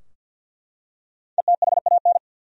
Morse code, heard as a keyed tone, about 51 wpm, 700 Hz. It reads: ETHAN
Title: USA Common Names - 50wpm